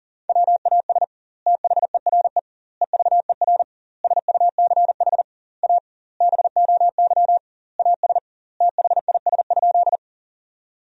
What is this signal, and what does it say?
Morse code, heard as a keyed tone, about 40 wpm, 700 Hz. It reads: WAS THERE EVER SUCH A BOY AS THIS?
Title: 2x Word-Spacing - Sentences from Top 400 Words - 40wpm